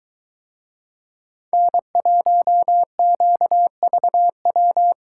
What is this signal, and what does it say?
Morse code, heard as a keyed tone, about 22 wpm, 700 Hz. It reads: N1QVW